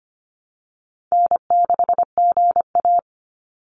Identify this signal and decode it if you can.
Morse code, heard as a keyed tone, about 25 wpm, 700 Hz. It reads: N6GA
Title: Easy Call Signs - 25wpm